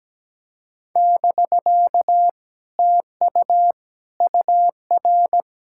Morse code, heard as a keyed tone, about 20 wpm, 700 Hz. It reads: <BK> TU UR
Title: Speed-Racing Common QSO Phrases - 17wpm